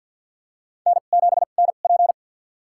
Morse code, heard as a keyed tone, about 52 wpm, spaced as for 36 wpm, 700 Hz. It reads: N7NP
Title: USA Call Signs - 4 Characters - 2x Character Spacing - 50wpm